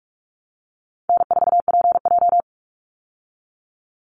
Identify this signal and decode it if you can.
Morse code, heard as a keyed tone, about 44 wpm, 700 Hz. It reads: N4PJ
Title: Call Signs Repeated 3x - 45wpm